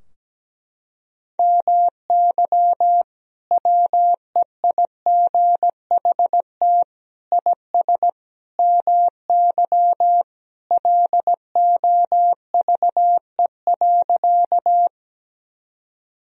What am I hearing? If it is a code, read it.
Morse code, heard as a keyed tone, about 17 wpm, 700 Hz. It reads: MY WEIGHT IS MY LOVE.
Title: Sentences from Top 1000 Words - Review Part 3 - 17wpm